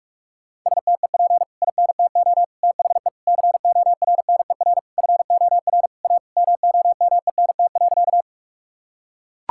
Morse code, heard as a keyed tone, about 41 wpm, 700 Hz. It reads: STEP INTO THE CORNER FOR A MOMENT.
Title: Sentences from Top 1600 Words 45wpm